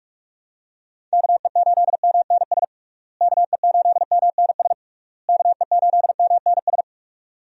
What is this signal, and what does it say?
Morse code, heard as a keyed tone, about 47 wpm, 700 Hz. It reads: KE8MNS KE8MNS KE8MNS
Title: USA Call Signs - 6 Characters - Repeated 3x - 45wpm